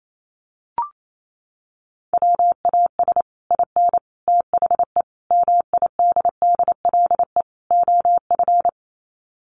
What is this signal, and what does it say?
Morse code, heard as a keyed tone, about 28 wpm, 690 Hz. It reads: WAS IN THE MIDDLE OF
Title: Sets of 5 Words Encore 28wpm